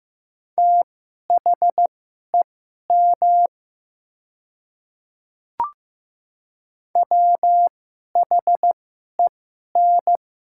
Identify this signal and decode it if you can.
Morse code, heard as a keyed tone, about 15 wpm, spaced as for 11 wpm, 700 Hz. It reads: THEM WHEN
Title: 2x Character Spacing - Top 100 Words - 15wpm